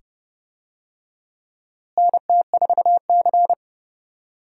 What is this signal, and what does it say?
Morse code, heard as a keyed tone, about 31 wpm, 700 Hz. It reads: NT4C
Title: USA Call Signs - 4 Characters - Repeated 2x - 30wpm